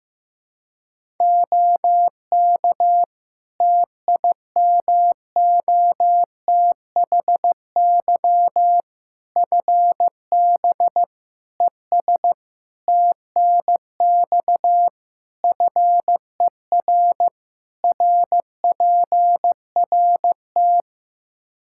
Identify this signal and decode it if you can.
Morse code, heard as a keyed tone, about 15 wpm, 700 Hz. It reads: OK TIMOTHY FB ES TNX FER RPRT
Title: Practice QSO - Exchange 2 - Info - 15wpm